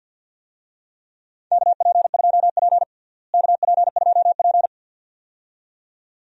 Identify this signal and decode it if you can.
Morse code, heard as a keyed tone, about 48 wpm, 700 Hz. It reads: KP2P KP2P
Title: DXCC-100 Call Signs - 4 Characters - Repeated 2x - 50wpm